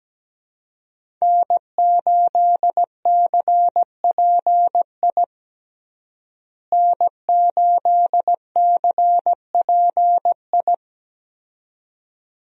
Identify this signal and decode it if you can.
Morse code, heard as a keyed tone, about 17 wpm, 700 Hz. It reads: N8CPI N8CPI
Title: USA Call Signs - 5 Characters - Repeated 2x - 17wpm